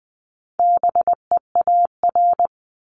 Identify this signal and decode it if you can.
Morse code, heard as a keyed tone, about 20 wpm, 700 Hz. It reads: BEAR
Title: ING Suffix - Top 100 Words with Base Words 20wpm